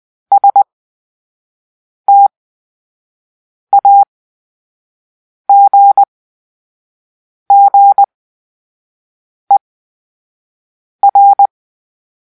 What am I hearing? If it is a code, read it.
Morse code, heard as a keyed tone, about 20 wpm, spaced as for 5 wpm, 800 Hz. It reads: STAGGER
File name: Grp-3 Drill-4